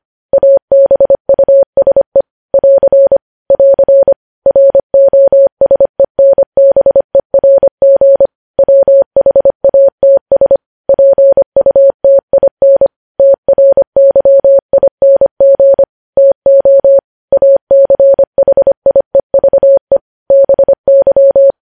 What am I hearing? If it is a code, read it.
Morse code, heard as a keyed tone, about 25 wpm, 550 Hz. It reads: ABUSE <AR> <AR> ROSENBERG WHATS PUTIN TRYING TO ACHIEVE BY
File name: podcast / News headlines-1x-25x25